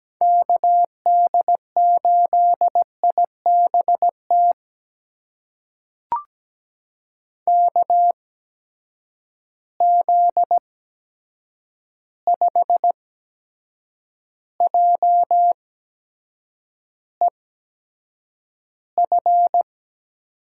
Morse code, heard as a keyed tone, about 17 wpm, spaced as for 7 wpm, 700 Hz. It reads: KD8IBT KZ5JEF
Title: USA Call Signs - 6 Characters - 8x Character Spacing - 17wpm